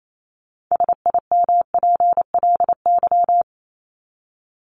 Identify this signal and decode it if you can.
Morse code, heard as a keyed tone, about 28 wpm, 700 Hz. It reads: SIMPLY
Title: Top 501-750 Words 28wpm